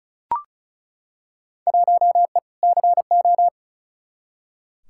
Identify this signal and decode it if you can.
Morse code, heard as a keyed tone, about 36 wpm, 700 Hz. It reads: 1E CO
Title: Field Day Exchanges - 35wpm